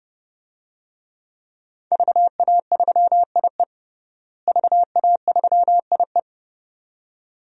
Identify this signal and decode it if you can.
Morse code, heard as a keyed tone, about 30 wpm, 700 Hz. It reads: VA3IE VA3IE
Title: DXCC-100 Call Signs - 5 Characters - Repeated 2x - 30wpm